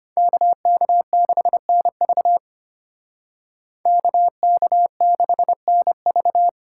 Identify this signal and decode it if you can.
Morse code, heard as a keyed tone, about 33 wpm, 700 Hz. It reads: KK6NV KK6NV